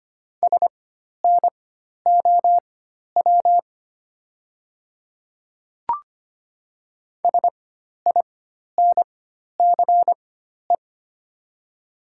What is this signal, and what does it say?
Morse code, heard as a keyed tone, about 25 wpm, spaced as for 12 wpm, 700 Hz. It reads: SNOW SINCE
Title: 4x Character Spacing - Top 500 Words - 25wpm